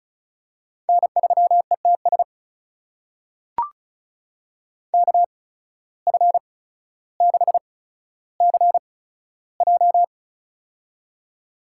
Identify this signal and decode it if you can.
Morse code, heard as a keyed tone, about 36 wpm, spaced as for 15 wpm, 700 Hz. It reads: N3ETS KF6CJ
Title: USA Call Signs - 5 Characters - 8x Character Spacing - 35wpm